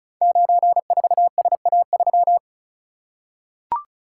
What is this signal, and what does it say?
Morse code, heard as a keyed tone, about 36 wpm, 700 Hz. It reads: 94SA3
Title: Code Groups - 5 Letter:Numbers - 35wpm